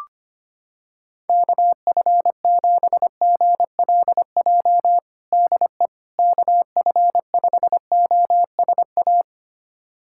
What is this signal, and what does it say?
Morse code, heard as a keyed tone, about 25 wpm, 700 Hz. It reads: KF7GLJ DE KF5OSA
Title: USA Callsign DE Callsign - 6 Characters - 25wpm